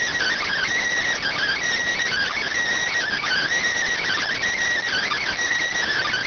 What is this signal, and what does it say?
Emergency Alert System (SAME) tones, heard as harsh buzzing bursts. It is heard from 1.2 to 5.3 s.